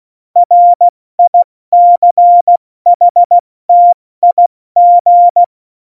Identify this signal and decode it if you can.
Morse code, heard as a keyed tone, about 16 wpm, 700 Hz. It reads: RICHTIG
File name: Morsekurs / Witze / Witz 16. Juli 2019, 70 F80